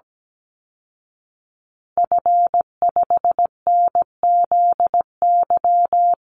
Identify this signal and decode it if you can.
Morse code, heard as a keyed tone, about 17 wpm, 700 Hz. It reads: F5NZY
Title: Call Signs Repeated 3x - 17wpm